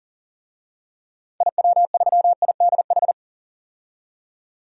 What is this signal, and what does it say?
Morse code, heard as a keyed tone, about 40 wpm, 700 Hz. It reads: IW3IDH
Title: DXCC-100 Call Signs - 6 Characters - 40wpm